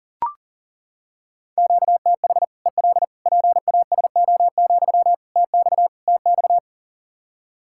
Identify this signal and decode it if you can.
Morse code, heard as a keyed tone, about 40 wpm, 700 Hz. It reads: QTH EL PASO, TX TX
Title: Practice QSO - Exchange 1 - Beginning - 40wpm